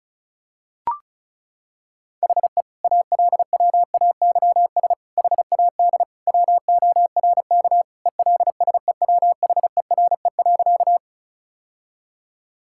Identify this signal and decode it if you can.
Morse code, heard as a keyed tone, about 36 wpm, 690 Hz. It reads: HE ALWAYS HAD WORK ELSEWHERE.
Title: Sentences from Top 1600 Words 35wpm